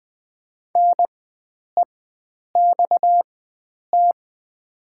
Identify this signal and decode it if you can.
Morse code, heard as a keyed tone, about 20 wpm, spaced as for 9 wpm, 700 Hz. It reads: NEXT
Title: 4x Character Spacing - Top 500 Words - 20wpm